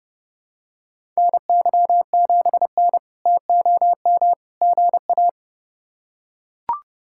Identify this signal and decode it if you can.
Morse code, heard as a keyed tone, about 32 wpm, 700 Hz. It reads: NY7N TOM GA
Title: K1USN SST QSO Practice - 30wpm